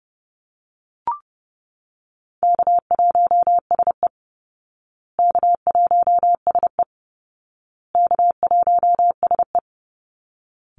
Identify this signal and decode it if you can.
Morse code, heard as a keyed tone, about 30 wpm, 700 Hz. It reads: K1SE K1SE K1SE
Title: Call Signs Repeated 2x - 30wpm